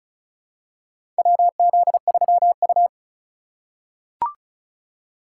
Morse code, heard as a keyed tone, about 36 wpm, 700 Hz. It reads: WZ3U
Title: USA Call Signs - 4 Characters - 6x Character Spacing - 35wpm